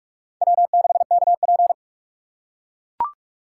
Morse code, heard as a keyed tone, about 41 wpm, 700 Hz. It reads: W6KP
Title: USA Call Signs - 4 Characters - 6x Character Spacing - 45wpm